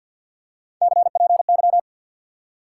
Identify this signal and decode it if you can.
Morse code, heard as a keyed tone, about 53 wpm, 700 Hz. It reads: CPY
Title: Top 180 QSO Elements 50wpm